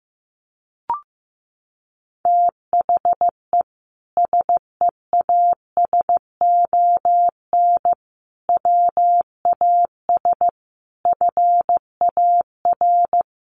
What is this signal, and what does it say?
Morse code, heard as a keyed tone, about 15 wpm, 700 Hz. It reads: THE SEASON WAS FAR
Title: Sentences from Top 1400 Words 15wpm